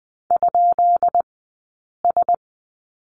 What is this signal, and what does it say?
Morse code, heard as a keyed tone, about 20 wpm, 700 Hz. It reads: ? S
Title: 132 Lesson 22d - Question Mark - Two Characters 20wpm